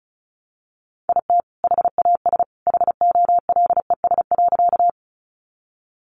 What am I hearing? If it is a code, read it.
Morse code, heard as a keyed tone, about 34 wpm, 700 Hz. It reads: IT HAS HOLES.
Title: Additional Easy Phrases 35wpm